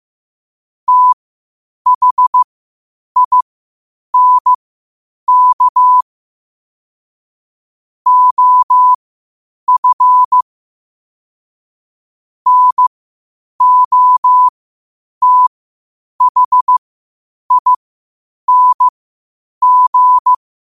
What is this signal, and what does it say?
Morse code, heard as a keyed tone, about 15 wpm, spaced as for 8 wpm, 1000 Hz. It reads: THINK OF NOTHING
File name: qotdfiles / 8wpm